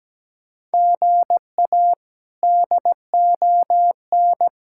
Morse code, heard as a keyed tone, about 17 wpm, 700 Hz. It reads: GA DON
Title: K1USN SST QSO Practice - 17wpm